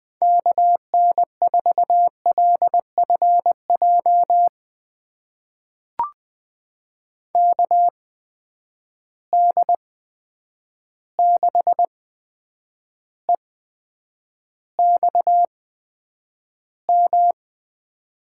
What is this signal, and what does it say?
Morse code, heard as a keyed tone, about 21 wpm, spaced as for 9 wpm, 700 Hz. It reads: KN4LFJ KD6EXM
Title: USA Call Signs - 6 Characters - 8x Character Spacing - 20wpm